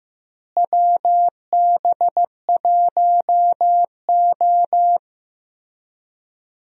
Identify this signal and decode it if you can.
Morse code, heard as a keyed tone, about 15 wpm, 700 Hz. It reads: WB1O
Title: USA Call Signs - 4 Characters - Repeated 3x - 15wpm